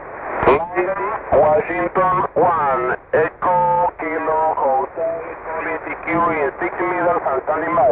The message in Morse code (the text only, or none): none